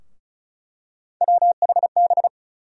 Morse code, heard as a keyed tone, about 36 wpm, 700 Hz. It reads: WHB
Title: USA Call Sign Postfixes - 3 Letters 35wpm